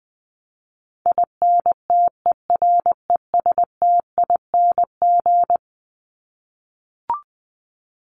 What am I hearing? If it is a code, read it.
Morse code, heard as a keyed tone, about 20 wpm, 700 Hz. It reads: INTERESTING
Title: ING Suffix - Top 100 Words with Base Words 20wpm